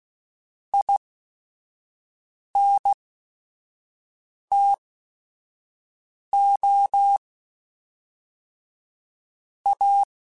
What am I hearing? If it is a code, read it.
Morse code, heard as a keyed tone, about 16 wpm, spaced as for 5 wpm, 800 Hz. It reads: INTO A
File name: QSOs / QSOs-mp3s / kc7sac@5